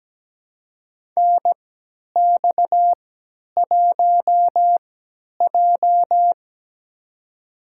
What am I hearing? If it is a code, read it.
Morse code, heard as a keyed tone, about 17 wpm, spaced as for 10 wpm, 700 Hz. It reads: NX1J
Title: USA Call Signs - 3x Character Spacing - 17wpm